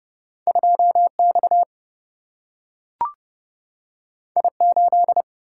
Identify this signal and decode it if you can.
Morse code, heard as a keyed tone, about 30 wpm, 700 Hz. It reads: 2X I8
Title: Single-Letter Single-Number 30wpm